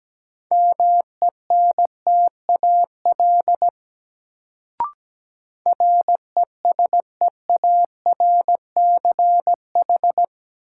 Morse code, heard as a keyed tone, about 17 wpm, 700 Hz. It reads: MENTAL RESEARCH
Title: Sentences from Top 1000 Words - Review Part 4 - New Word Drill - 17wpm